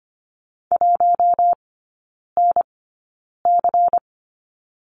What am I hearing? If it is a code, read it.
Morse code, heard as a keyed tone, about 25 wpm, spaced as for 9 wpm, 700 Hz. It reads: 1NC
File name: course / 25:10wpm / 093 Lesson 16e - Letter c - Three Characters 25wpm